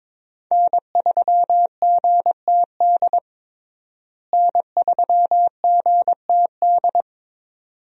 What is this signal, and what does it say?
Morse code, heard as a keyed tone, about 22 wpm, 700 Hz. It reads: N3GTD N3GTD
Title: USA Call Signs - 5 Characters - Repeated 2x - 22wpm